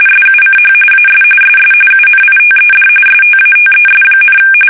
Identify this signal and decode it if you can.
RTTY, about 45.45 baud. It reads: SUCH AS SS UNITED STATES,
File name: fsk850 / battleshipnj